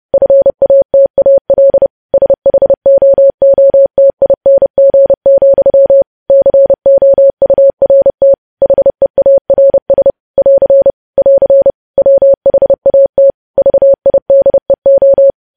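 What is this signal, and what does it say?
Morse code, heard as a keyed tone, about 30 wpm, 550 Hz. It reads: FATAL SHOOTING, COURT HEARS <AR> <AR> WHAT VIDEO
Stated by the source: Morse code news headlines sent at 30 words per minute.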